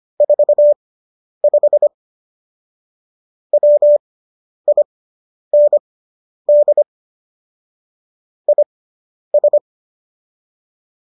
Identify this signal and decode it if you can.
Morse code, heard as a keyed tone, about 25 wpm, spaced as for 10 wpm, 590 Hz. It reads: 45 WIND IS